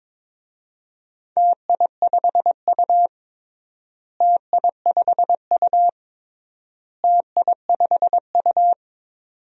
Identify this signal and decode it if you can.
Morse code, heard as a keyed tone, about 22 wpm, 700 Hz. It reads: TI5U TI5U TI5U
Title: DXCC-100 Call Signs - 4 Characters - Repeated 3x - 22wpm